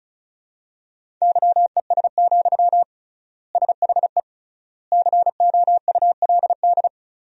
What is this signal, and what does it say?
Morse code, heard as a keyed tone, about 36 wpm, 700 Hz. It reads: YES, SHE COULD
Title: 3x Word-Spacing - Sentences from Top 500 Words - 35wpm